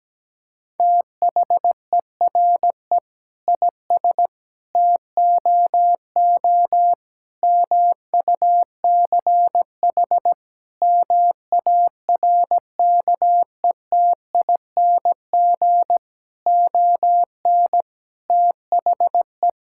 Morse code, heard as a keyed tone, about 17 wpm, 700 Hz. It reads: THERE IS TOO MUCH MARKETING ON THE
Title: Sentences from Top 1000 Words - Review Part 5 - 17wpm